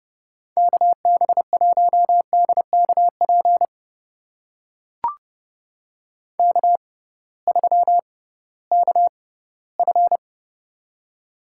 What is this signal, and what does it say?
Morse code, heard as a keyed tone, about 30 wpm, spaced as for 17 wpm, 700 Hz. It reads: KB1DKP K3KF
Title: USA Call Signs - 6x Character Spacing - 30wpm